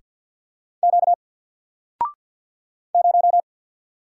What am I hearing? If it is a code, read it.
Morse code, heard as a keyed tone, about 50 wpm, 700 Hz. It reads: Q 0
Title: Single Letter-Number Alt 50wpm